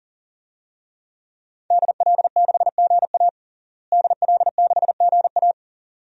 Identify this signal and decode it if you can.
Morse code, heard as a keyed tone, about 40 wpm, 700 Hz. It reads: DL6GA DL6GA
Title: DXCC-100 Call Signs - 5 Characters - Repeated 2x - 40wpm